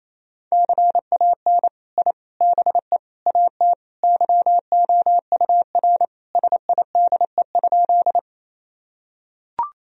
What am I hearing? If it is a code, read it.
Morse code, heard as a keyed tone, about 27 wpm, 700 Hz. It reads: CAN I BE AT YOUR SIDE?
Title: Sentences from Top 1000 Words - Review Part 1 - 28wpm